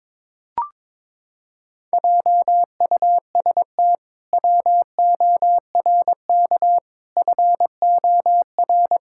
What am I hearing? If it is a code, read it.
Morse code, heard as a keyed tone, about 22 wpm, 700 Hz. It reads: JUST WORK FOR
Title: Pangram Challenge v2 - 22wpm